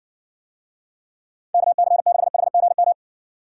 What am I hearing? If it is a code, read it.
Morse code, heard as a keyed tone, about 60 wpm, 700 Hz. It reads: KC6HGM